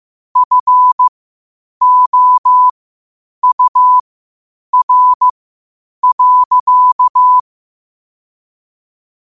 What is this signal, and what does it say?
Morse code, heard as a keyed tone, about 15 wpm, spaced as for 8 wpm, 1000 Hz. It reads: FOUR.
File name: qotdfiles / 8wpm